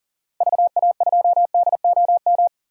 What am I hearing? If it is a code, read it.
Morse code, heard as a keyed tone, about 40 wpm, 700 Hz. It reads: VA2DOM
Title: DXCC-100 Call Signs - 6 Characters - Repeated 3x - 40wpm